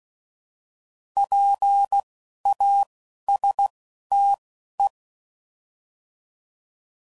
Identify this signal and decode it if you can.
Morse code, heard as a keyed tone, about 16 wpm, spaced as for 11 wpm, 800 Hz. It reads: PASTE
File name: Prefixes / By First Letter / Ps